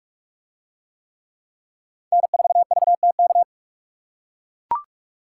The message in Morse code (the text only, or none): N4VTK